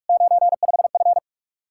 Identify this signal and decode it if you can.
Morse code, heard as a keyed tone, about 62 wpm, spaced as for 53 wpm, 700 Hz. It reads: 95F
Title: Speed-Racing Common QSO Phrases - 45wpm